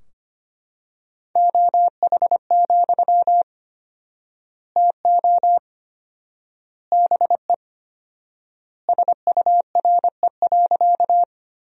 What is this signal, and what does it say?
Morse code, heard as a keyed tone, about 25 wpm, 700 Hz. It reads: OH, TO BE SURE.
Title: 4x Word-Spacing - Sentences from Top 500 Words - 25wpm